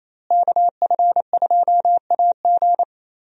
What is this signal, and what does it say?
Morse code, heard as a keyed tone, about 28 wpm, 700 Hz. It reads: KF2AG
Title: USA Call Signs - 5 Characters - Repeated 3x - 28wpm